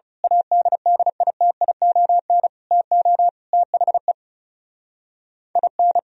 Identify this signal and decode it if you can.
Morse code, heard as a keyed tone, about 33 wpm, 700 Hz. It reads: ADDITION TO THE IN
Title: Speed-Racing - Sets of 4 Words 30wpm